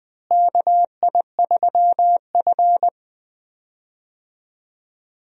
Morse code, heard as a keyed tone, about 20 wpm, 700 Hz. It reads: KI3F
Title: USA Call Signs - 4 Characters - Repeated 2x - 20wpm